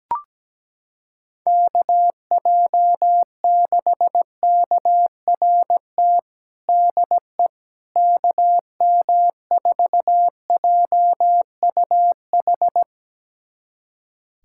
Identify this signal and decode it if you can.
Morse code, heard as a keyed tone, about 17 wpm, 700 Hz. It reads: KJ6KRT DE KM4JUH
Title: USA Callsign DE Callsign 17wpm